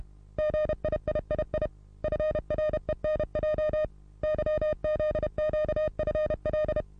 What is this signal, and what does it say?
Morse code, heard as a keyed tone, about 31 wpm, 610 Hz. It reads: GIIII FRENJ YZQFL
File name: repos / Morsekurs / DJ4UF Morsen27